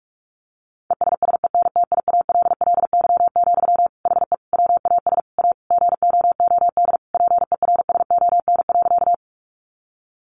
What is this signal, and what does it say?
Morse code, heard as a keyed tone, about 44 wpm, 700 Hz. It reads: ESSENTIALLY, HE WAS A GOOD PERSON.
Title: Sentences from Top 1300 Words 45wpm